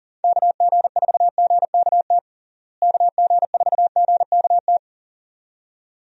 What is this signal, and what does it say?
Morse code, heard as a keyed tone, about 38 wpm, 700 Hz. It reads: KG4GKT KG4GKT
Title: USA Call Signs - 6 Characters - Repeated 2x - 40wpm